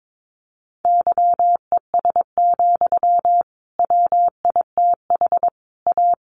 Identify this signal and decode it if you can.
Morse code, heard as a keyed tone, about 22 wpm, 700 Hz. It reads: YES, WITH A
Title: Sentences from Top 1300 Words 22wpm